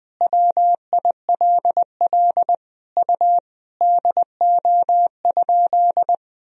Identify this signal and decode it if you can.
Morse code, heard as a keyed tone, about 20 wpm, 700 Hz. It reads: WILL U DO?
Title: Sentences from Top 100 Words and Abbreviations 20wpm